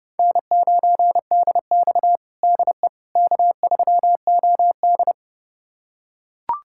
Morse code, heard as a keyed tone, about 32 wpm, 700 Hz. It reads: N9DX DE K3OD
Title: USA Callsign DE Callsign - 4 Characters - 30wpm